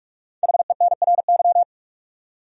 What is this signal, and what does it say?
Morse code, heard as a keyed tone, about 46 wpm, 700 Hz. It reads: HENRY
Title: USA Common Names - Max 5 Letters - 45wpm